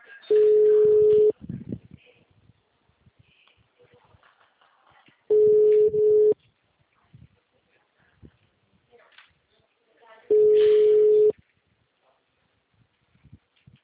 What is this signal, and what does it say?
Ringback tone (Europe, 425 Hz), with speech.